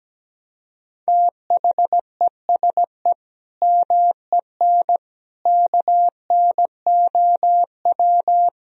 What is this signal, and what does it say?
Morse code, heard as a keyed tone, about 17 wpm, 700 Hz. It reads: THESE MEN KNOW
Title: Sentences from Top 1000 Words - Review Part 1 - 17wpm